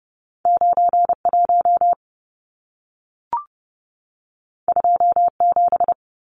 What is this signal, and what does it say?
Morse code, heard as a keyed tone, about 30 wpm, 700 Hz. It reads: 91 27
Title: Numbers - 2-Digit 30wpm